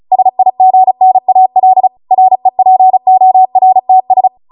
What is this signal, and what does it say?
Morse code, heard as a keyed tone, about 36 wpm, 750 Hz. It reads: SIGNAL REPORTS
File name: Morse / 35WPM